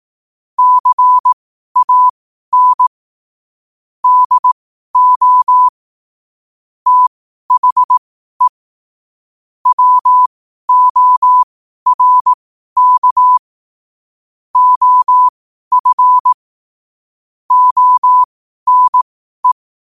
Morse code, heard as a keyed tone, about 18 wpm, spaced as for 12 wpm, 1000 Hz. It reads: CAN DO THE WORK OF ONE
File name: qotdfiles / 12wpm